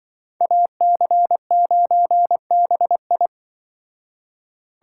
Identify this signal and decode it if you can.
Morse code, heard as a keyed tone, about 24 wpm, 700 Hz. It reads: AC9BI
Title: Speed Racing - USA 2x2 Call Signs 20wpm